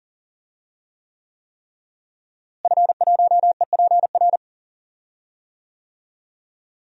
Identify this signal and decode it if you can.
Morse code, heard as a keyed tone, about 37 wpm, 700 Hz. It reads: F1EPR
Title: DXCC-100 Call Signs - 5 Characters - Repeated 3x - 40wpm